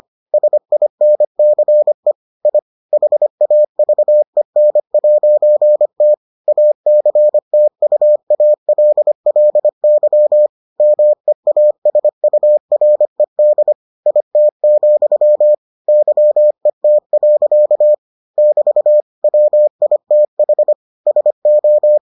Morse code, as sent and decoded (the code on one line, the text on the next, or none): ... .. -. -.-. / .. / .... .- ...- . -. .----. - / .- -.-. - ..- .- .-.. .-.. -.-- / -- . .- ... ..- .-. -.. / .. - --..-- / -.-- . - .-.-.- / -...- / .-- .. - .... / ... ---
SINCE I HAVEN'T ACTUALLY MEASURED IT, YET. <BT> WITH SO